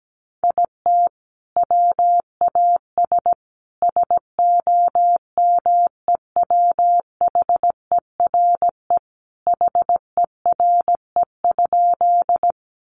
Morse code, heard as a keyed tone, about 17 wpm, 700 Hz. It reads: IT WAS SOMEWHERE HERE?
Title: Sentences from Top 1400 Words 17wpm